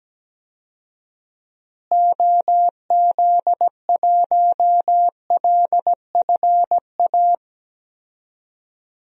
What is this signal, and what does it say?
Morse code, heard as a keyed tone, about 17 wpm, 700 Hz. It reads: OZ1LFA
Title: DXCC-100 Call Signs - 6 Characters - Repeated 3x - 17wpm